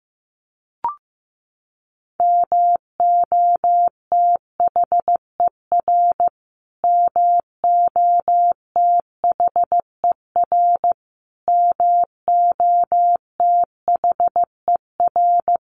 Morse code, heard as a keyed tone, about 15 wpm, 700 Hz. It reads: MOTHER MOTHER MOTHER
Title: Top 500 Words Repeated 15wpm